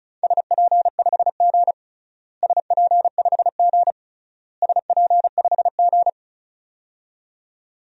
Morse code, heard as a keyed tone, about 36 wpm, 700 Hz. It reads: SP5G SP5G SP5G
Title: DXCC-100 Call Signs - 4 Characters - Repeated 3x - 35wpm